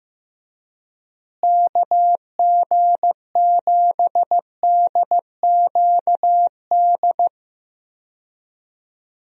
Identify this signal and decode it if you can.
Morse code, heard as a keyed tone, about 15 wpm, 700 Hz. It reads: KG7DQD
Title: USA Call Signs - 6 Characters - Repeated 2x - 15wpm